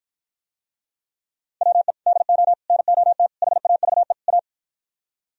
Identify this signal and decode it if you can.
Morse code, heard as a keyed tone, about 52 wpm, 700 Hz. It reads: WE DO NOT HAVE A